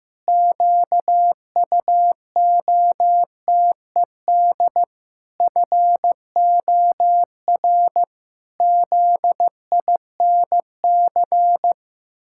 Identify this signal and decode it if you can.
Morse code, heard as a keyed tone, about 15 wpm, 700 Hz. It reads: QUOTED FOR ZINC
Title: Pangram Challenge v2 - 15wpm